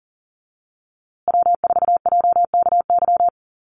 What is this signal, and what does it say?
Morse code, heard as a keyed tone, about 40 wpm, 700 Hz. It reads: W4JKY
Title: USA Call Signs Repeated 2x 40wpm